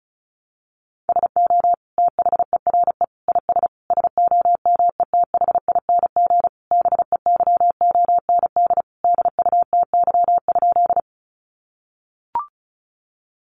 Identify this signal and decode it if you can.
Morse code, heard as a keyed tone, about 34 wpm, 700 Hz. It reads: SO THERE IS SOMETHING BEYOND DUTY?
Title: Sentences from Top 1500 Words 35wpm